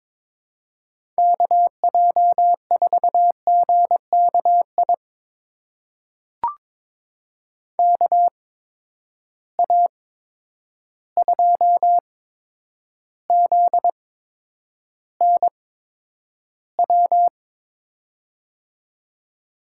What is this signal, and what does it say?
Morse code, heard as a keyed tone, about 23 wpm, spaced as for 10 wpm, 700 Hz. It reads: KJ4GKI KA2ZNW
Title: USA Call Signs - 6 Characters - 8x Character Spacing - 22wpm